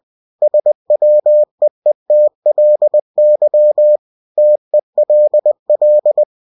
Morse code, heard as a keyed tone, about 20 wpm, 600 Hz. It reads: SWEETLY TELL